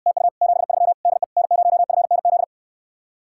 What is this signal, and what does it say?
Morse code, heard as a keyed tone, about 69 wpm, 700 Hz. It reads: TU 73 DE N0FNZ